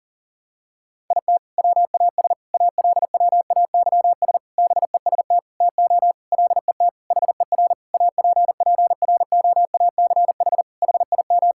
Morse code, heard as a keyed tone, about 37 wpm, 700 Hz. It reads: IT WAS ALWAYS BEST TO LET HER APPROACH HIM
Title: Sentences from Top 1000 Words - Review Part 6 - 40wpm